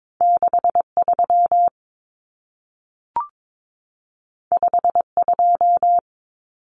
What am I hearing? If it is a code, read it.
Morse code, heard as a keyed tone, about 22 wpm, 700 Hz. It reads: 63 52
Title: Numbers - 2-Digit 22wpm